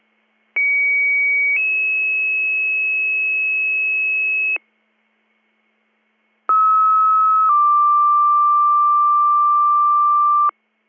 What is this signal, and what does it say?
Two-tone page, heard as a one-second tone, then a three-second tone, about 2250 then 2600 Hz, about 1290 then 1180 Hz. It is heard from 0.6 to 10.5 s.